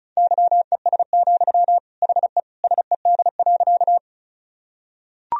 Morse code, heard as a keyed tone, about 36 wpm, 700 Hz. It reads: YES, HE SED.
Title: Sentences from Top 500 Words and Abbreviations 35wpm